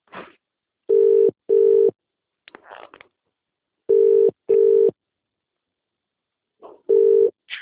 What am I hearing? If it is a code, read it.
Ringback tone (UK), 400 + 450 Hz.